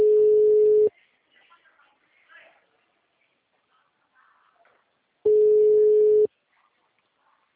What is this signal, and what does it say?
Ringback tone (Europe, 425 Hz).